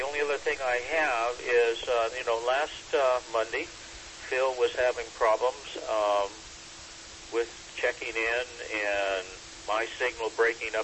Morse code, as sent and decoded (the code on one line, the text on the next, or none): none